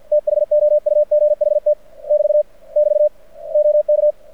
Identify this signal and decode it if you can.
Morse code, heard as a keyed tone, about 52 wpm, 600 Hz. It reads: THOUGHT <BT> <BT> OK
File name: archived / CRSnet